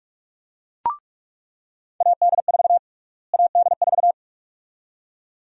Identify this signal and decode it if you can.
Morse code, heard as a keyed tone, about 42 wpm, 700 Hz. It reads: AD4 AD4
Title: USA Call Sign Prefixes - Repeated 2x - 45wpm